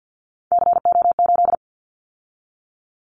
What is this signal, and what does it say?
Morse code, heard as a keyed tone, about 50 wpm, 690 Hz. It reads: CO8
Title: Callsign Prefixes for Top 100 DXCC - 50wpm